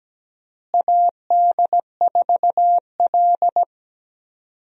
Morse code, heard as a keyed tone, about 17 wpm, 700 Hz. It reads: AD4L
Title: USA Call Signs - 4 Characters - Repeated 3x - 17wpm